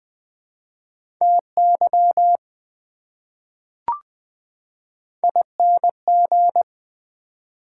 Morse code, heard as a keyed tone, about 20 wpm, 700 Hz. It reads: TY ING
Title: Prefixes and Suffixes 20wpm